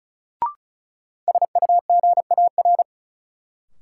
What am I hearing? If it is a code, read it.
Morse code, heard as a keyed tone, about 36 wpm, 700 Hz. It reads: SUGAR
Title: Sentences from Top 1200 Words - New Word Drill Alt - 35wpm